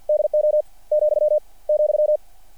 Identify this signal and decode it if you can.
Morse code, heard as a keyed tone, about 48 wpm, 600 Hz. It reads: DO , ,
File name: archived / CRSnet